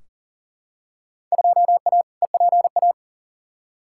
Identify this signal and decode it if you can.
Morse code, heard as a keyed tone, about 40 wpm, 700 Hz. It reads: 2A EPA
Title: Field Day Exchanges - 40wpm